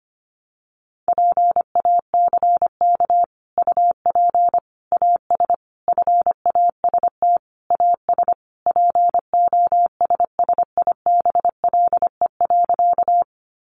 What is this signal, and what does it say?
Morse code, heard as a keyed tone, about 25 wpm, 700 Hz. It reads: PACK UP AS FAST AS POSSIBLE.
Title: Sentences from Top 1300 Words 25wpm